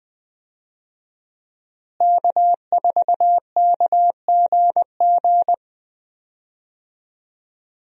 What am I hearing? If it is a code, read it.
Morse code, heard as a keyed tone, about 20 wpm, 700 Hz. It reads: K4KGG
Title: USA Call Signs - 5 Characters - Repeated 2x - 20wpm